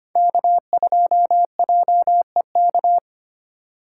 Morse code, heard as a keyed tone, about 25 wpm, 700 Hz. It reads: K2JEK
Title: USA Call Signs - 5 Characters - 2x Character Spacing - 25wpm